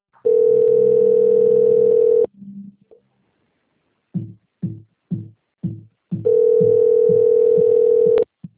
Ringback tone (North America), 440 + 480 Hz.
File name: out- - - - - .14459